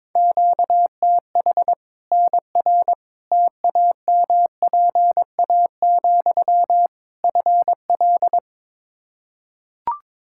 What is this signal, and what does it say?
Morse code, heard as a keyed tone, about 21 wpm, 700 Hz. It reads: QTH NR TAMPA, FL
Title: Practice QSO - Exchange 1 - Beginning - 22wpm